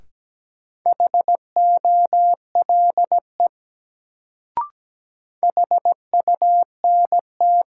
Morse code, heard as a keyed tone, about 17 wpm, 700 Hz. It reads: HOLE HUNT
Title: Sentences from Top 1600 Words - New Word Drill - 17wpm